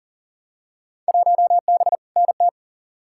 Morse code, heard as a keyed tone, about 40 wpm, 700 Hz. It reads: 1B NT
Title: Field Day Exchanges Encore - 40wpm